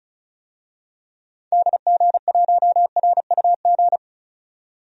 Morse code, heard as a keyed tone, about 36 wpm, 700 Hz. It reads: DG1RUG
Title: DXCC-100 Call Signs - 6 Characters - 35wpm